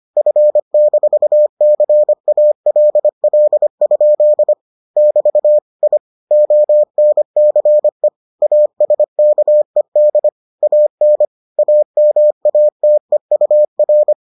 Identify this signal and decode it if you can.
Morse code, heard as a keyed tone, about 25 wpm, 600 Hz. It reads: F-CALL? <BT> I ONCE ASKED AN AMATEUR